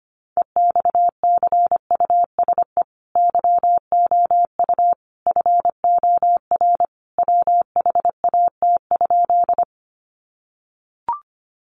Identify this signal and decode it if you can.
Morse code, heard as a keyed tone, about 25 wpm, 700 Hz. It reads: EXCUSE YOU FOR WHAT?
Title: Sentences from Top 700 Words 25wpm